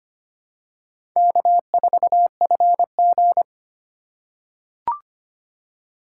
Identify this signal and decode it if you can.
Morse code, heard as a keyed tone, about 25 wpm, 700 Hz. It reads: K4FG
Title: USA Call Signs - 4x Character Spacing - 25wpm